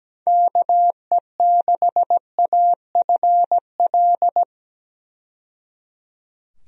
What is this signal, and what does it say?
Morse code, heard as a keyed tone, about 17 wpm, 700 Hz. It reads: KE6AFL
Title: USA Call Signs - 6 Characters - 17wpm